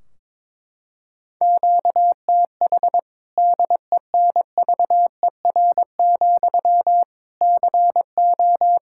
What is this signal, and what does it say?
Morse code, heard as a keyed tone, about 22 wpm, 700 Hz. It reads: QTH DENVER, CO
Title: K1USN SST QSO Practice 22wpm